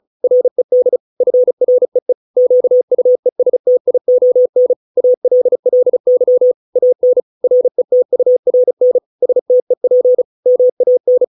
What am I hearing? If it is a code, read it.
Morse code, heard as a keyed tone, about 34 wpm, 500 Hz. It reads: RED FREE QUESTION ALLY AN RETURN STEP MAN